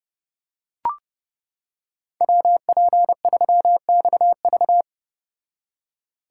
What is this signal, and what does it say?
Morse code, heard as a keyed tone, about 29 wpm, 700 Hz. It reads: WP3XV